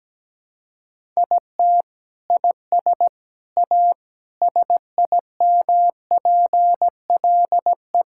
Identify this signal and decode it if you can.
Morse code, heard as a keyed tone, about 17 wpm, 700 Hz. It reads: IT IS A SIMPLE
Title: Sentences from Top 1000 Words - Review Part 6 - 17wpm